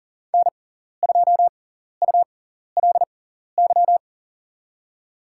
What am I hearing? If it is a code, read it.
Morse code, heard as a keyed tone, about 40 wpm, spaced as for 14 wpm, 700 Hz. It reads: N2ULY
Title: USA Call Signs - 5 Characters - 6x Character Spacing - 40wpm